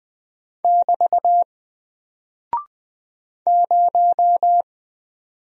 Morse code, heard as a keyed tone, about 20 wpm, 700 Hz. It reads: <BT> 0
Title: Single Character (Letter:Number:Punctuation:Prosign) Alt 20wpm